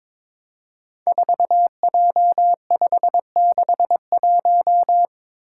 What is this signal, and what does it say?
Morse code, heard as a keyed tone, about 22 wpm, 700 Hz. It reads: 4J561
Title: Five Challenging Letter-Number Combinations 22wpm